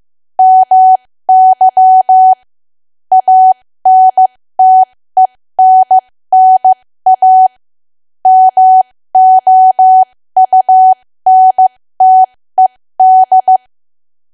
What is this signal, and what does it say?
Morse code, heard as a keyed tone, about 15 wpm, spaced as for 13 wpm, 750 Hz. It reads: MY ANTENNA MOUNTED
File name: cwa / W1AW-Level2 / 13WPM